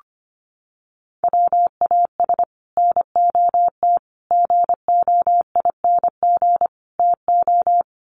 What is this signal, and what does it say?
Morse code, heard as a keyed tone, about 25 wpm, 700 Hz. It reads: WAS NOT GOING TO
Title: Sets of 4 Words 25wpm